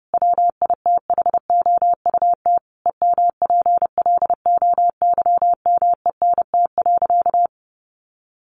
Morse code, heard as a keyed tone, about 30 wpm, 700 Hz. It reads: WITHOUT EMPLOYMENT.
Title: Sentences from Top 1500 Words 30wpm